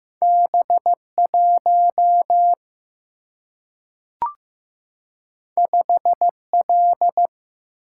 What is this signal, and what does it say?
Morse code, heard as a keyed tone, about 15 wpm, 700 Hz. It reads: B 1 5 L
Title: Single-Letter Single-Number 15wpm